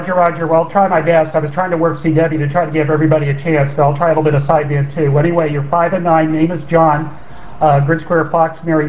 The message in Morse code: none